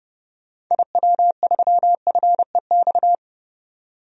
Morse code, heard as a keyed tone, about 30 wpm, 700 Hz. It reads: IW3FEX
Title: DXCC-100 Call Signs - 6 Characters - 30wpm